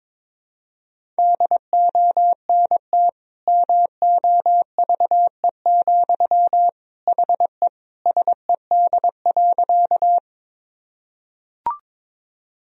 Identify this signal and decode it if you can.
Morse code, heard as a keyed tone, about 22 wpm, 700 Hz. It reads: DONT MOVE, HE SED.
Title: Sentences from Top 300 Words and Abbreviations 22wpm